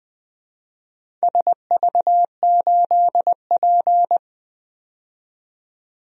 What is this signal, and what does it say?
Morse code, heard as a keyed tone, about 20 wpm, 700 Hz. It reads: SV8P
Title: DXCC-100 Call Signs - 4 Characters - 20wpm